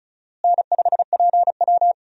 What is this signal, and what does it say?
Morse code, heard as a keyed tone, about 36 wpm, 700 Hz. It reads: N5PW
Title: USA Call Signs - 4 Characters - Repeated 2x - 35wpm